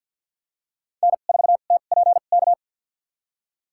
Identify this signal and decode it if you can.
Morse code, heard as a keyed tone, about 52 wpm, spaced as for 36 wpm, 700 Hz. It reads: N4TPK
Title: USA Call Signs - 5 Characters - 2x Character Spacing - 50wpm